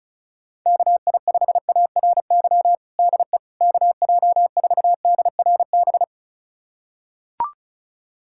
Morse code, heard as a keyed tone, about 36 wpm, 700 Hz. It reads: KI5ARY DE KJ4DRB
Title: USA Callsign DE Callsign 35wpm